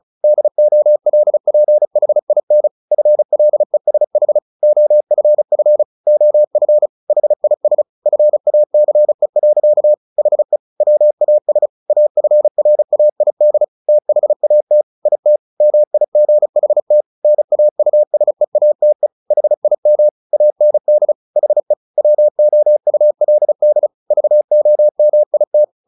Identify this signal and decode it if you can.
Morse code, heard as a keyed tone, about 34 wpm, 600 Hz. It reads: DOLPHIN FLESH OFF OF HIS FACE. HE WAS AFRAID THAT IT MIGHT NAUSEATE HIM AND HE WOULD VOMIT
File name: the old man and the sea 2 35wpm